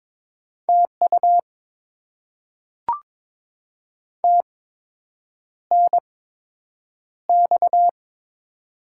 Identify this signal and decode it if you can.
Morse code, heard as a keyed tone, about 22 wpm, spaced as for 8 wpm, 700 Hz. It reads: TU TNX
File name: Top 20 QSO Elements - 8x Character Spacing - 22wpm